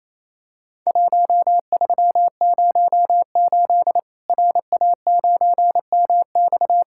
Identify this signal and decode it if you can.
Morse code, heard as a keyed tone, about 28 wpm, 700 Hz. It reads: 1308 RA9MX
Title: Worked All Europe DX Contest - QTC Exchange - 28wpm